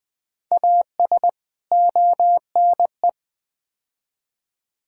Morse code, heard as a keyed tone, about 20 wpm, 700 Hz. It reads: AS ONE
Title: Build A Phrase - 4-Word Phrases - 20wpm